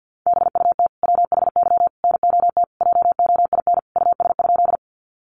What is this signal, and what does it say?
Morse code, heard as a keyed tone, about 50 wpm, 700 Hz. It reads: BUT WHY NOT JOIN US?
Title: Sentences from Top 1400 Words 50wpm